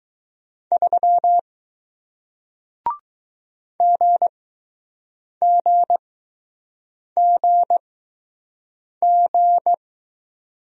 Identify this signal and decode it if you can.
Morse code, heard as a keyed tone, about 20 wpm, spaced as for 6 wpm, 700 Hz. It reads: 3 GGGG